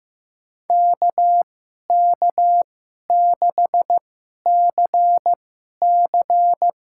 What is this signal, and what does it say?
Morse code, heard as a keyed tone, about 15 wpm, spaced as for 11 wpm, 700 Hz. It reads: KK6CC
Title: USA Call Signs - 2x Character Spacing - 15wpm